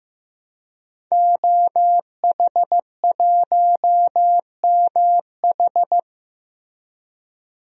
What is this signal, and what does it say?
Morse code, heard as a keyed tone, about 15 wpm, 700 Hz. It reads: OH1MH
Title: DXCC-100 Call Signs - 5 Characters - Repeated 2x - 15wpm